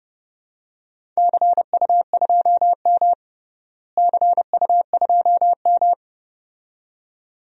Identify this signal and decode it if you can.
Morse code, heard as a keyed tone, about 30 wpm, 700 Hz. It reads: CU2M CU2M
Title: DXCC-100 Call Signs - 4 Characters - Repeated 2x - 30wpm